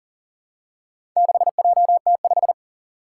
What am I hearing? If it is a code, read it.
Morse code, heard as a keyed tone, about 43 wpm, 700 Hz. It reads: 6JT5
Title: Four Challenging Letter-Number Combinations 40wpm